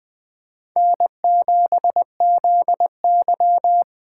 Morse code, heard as a keyed tone, about 20 wpm, 700 Hz. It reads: N7ZY
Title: USA Call Signs - 4 Characters - 4x Character Spacing - 20wpm